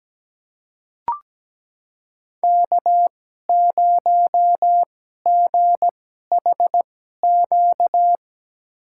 Morse code, heard as a keyed tone, about 17 wpm, spaced as for 12 wpm, 700 Hz. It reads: K0GHQ
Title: USA Call Signs - 5 Characters - 2x Character Spacing - 17wpm